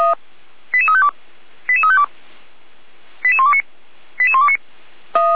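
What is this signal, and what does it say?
Five-tone selective call (ZVEI), heard as quick runs of five tones, tones of about 70 ms. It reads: 88352 88352 88129 88129
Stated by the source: ZVEI five-tone selective call received with an SDR, from the Orlamünde fire brigade (feuerwehr-orlamuende) audio server f85360.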